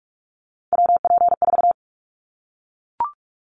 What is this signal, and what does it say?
Morse code, heard as a keyed tone, about 44 wpm, 700 Hz. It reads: WP4
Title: Callsign Prefixes for Top 25 DXCC - 45wpm